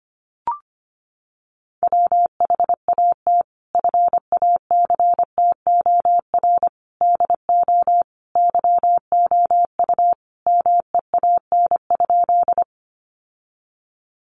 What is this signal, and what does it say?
Morse code, heard as a keyed tone, about 25 wpm, 700 Hz. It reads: WHAT FACTOR DO YOU MEAN?
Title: Sentences from Top 1300 Words 25wpm